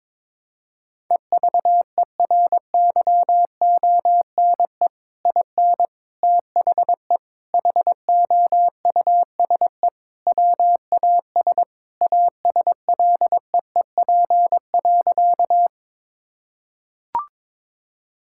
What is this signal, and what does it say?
Morse code, heard as a keyed tone, about 22 wpm, 700 Hz. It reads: EVERYONE IN THE HOUSE WAS ASLEEP.
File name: practice / 22wpm / Sentences with 3-Word Phrases - 22wpm